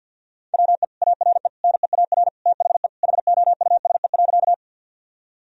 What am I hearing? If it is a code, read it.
Morse code, heard as a keyed tone, about 47 wpm, 700 Hz. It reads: WE ARE NEAR THE HOUSE.
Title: Sentences from Top 1000 Words - Review Part 1 - 50wpm